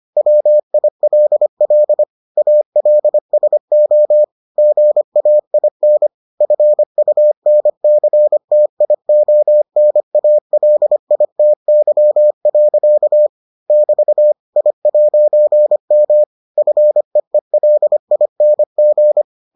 Morse code, heard as a keyed tone, about 25 wpm, 600 Hz. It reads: WILL ALSO GAIN FUNCTIONALITY. <BT> I'M FEELING